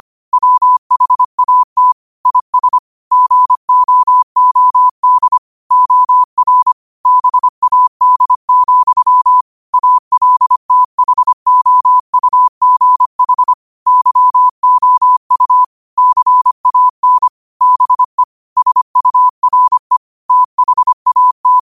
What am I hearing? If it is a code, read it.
Morse code, heard as a keyed tone, about 25 wpm, 1000 Hz. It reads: WHAT IS GOOD OR BAD, ALTHOUGH YOU CAN BE SURE THAT